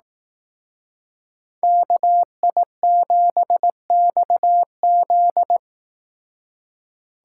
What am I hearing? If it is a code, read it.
Morse code, heard as a keyed tone, about 18 wpm, 700 Hz. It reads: KI7XZ
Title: Speed Racing - USA 2x2 Call Signs 15wpm